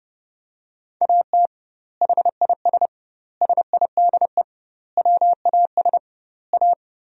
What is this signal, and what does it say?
Morse code, heard as a keyed tone, about 30 wpm, 700 Hz. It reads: AT HIS SIDE WAS A
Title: 2x Word-Spacing - Sentences from Top 500 Words - 30wpm